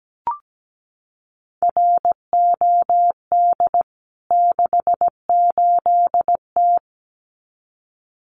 Morse code, heard as a keyed tone, about 17 wpm, 700 Hz. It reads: ROD 68T
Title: CWT Exchange 17wpm